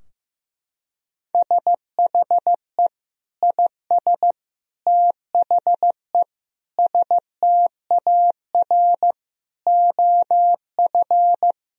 Morse code, heard as a keyed tone, about 15 wpm, 700 Hz. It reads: SHE IS THE STAR OF
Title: Sentences from Top 1000 Words - Review Part 2 - 15wpm